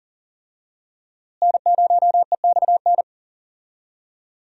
Morse code, heard as a keyed tone, about 40 wpm, 700 Hz. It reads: N0EXN
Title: USA Call Signs - 5 Characters - Repeated 2x - 40wpm